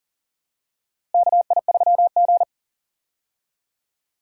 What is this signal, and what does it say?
Morse code, heard as a keyed tone, about 40 wpm, 700 Hz. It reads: KI3G
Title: USA Call Signs - 4 Characters - Repeated 3x - 40wpm